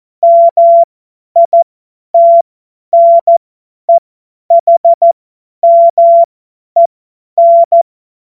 Morse code, heard as a keyed tone, about 14 wpm, spaced as for 10 wpm, 680 Hz. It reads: MITNEHMEN
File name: Morsekurs / Witze / Witz 07.November 2023 50 F70